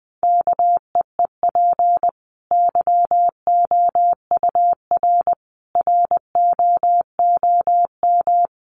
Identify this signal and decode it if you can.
Morse code, heard as a keyed tone, about 20 wpm, 700 Hz. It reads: KEEP YOUR ROOM
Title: Sentences from Top 1100 Words 20wpm